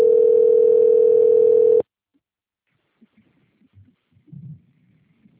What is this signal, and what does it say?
Ringback tone (North America), 440 + 480 Hz.
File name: out- - - - - .44943